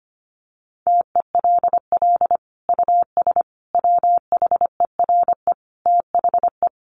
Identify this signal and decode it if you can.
Morse code, heard as a keyed tone, about 25 wpm, 700 Hz. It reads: TELL US WHERE THE
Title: Sentences from Top 1200 Words 25wpm